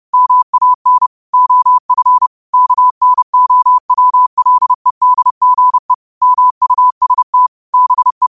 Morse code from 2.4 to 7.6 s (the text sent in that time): KNOWLEDGE MUST